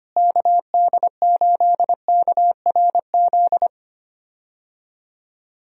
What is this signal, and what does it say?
Morse code, heard as a keyed tone, about 25 wpm, 700 Hz. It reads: KD8KRZ
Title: USA Call Signs - 6 Characters - Repeated 2x - 25wpm